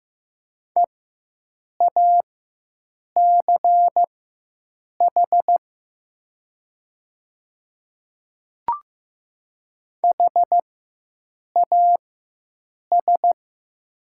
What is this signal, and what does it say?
Morse code, heard as a keyed tone, about 15 wpm, spaced as for 7 wpm, 700 Hz. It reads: EACH HAS
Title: 4x Character Spacing - Top 100 Words - 15wpm